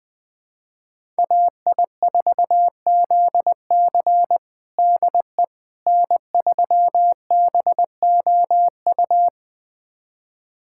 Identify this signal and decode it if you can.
Morse code, heard as a keyed tone, about 21 wpm, 700 Hz. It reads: AI4ZC DE N3BOU
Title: USA Callsign DE Callsign - 5 Characters - 20wpm